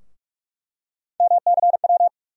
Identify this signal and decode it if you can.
Morse code, heard as a keyed tone, about 42 wpm, 700 Hz. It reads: MCW
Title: USA Call Sign Postfixes - 3 Letters 45wpm